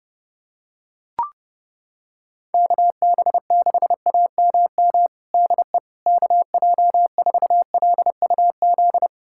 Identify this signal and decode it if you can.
Morse code, heard as a keyed tone, about 30 wpm, 700 Hz. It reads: KB6AMM DE KJ4LUZ
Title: USA Callsign DE Callsign - 6 Characters - 30wpm